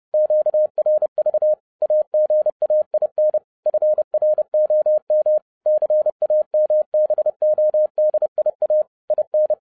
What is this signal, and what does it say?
Morse code, heard as a keyed tone, about 30 wpm, 610 Hz. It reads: QRV AGAIN FROM CAMBODIA IN